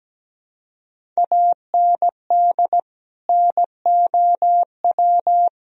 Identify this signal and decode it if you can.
Morse code, heard as a keyed tone, about 17 wpm, 700 Hz. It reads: AND NOW
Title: Sentences from Top 1000 Words - Review Part 1 - 17wpm